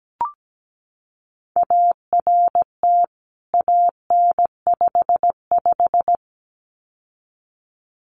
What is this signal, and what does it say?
Morse code, heard as a keyed tone, about 17 wpm, 700 Hz. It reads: ART AN55
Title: CWT Exchange 17wpm